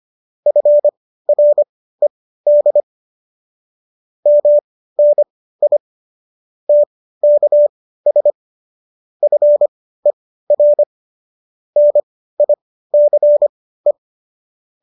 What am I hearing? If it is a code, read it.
Morse code, heard as a keyed tone, about 25 wpm, spaced as for 15 wpm, 590 Hz. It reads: FRED MNI TKS FER NICE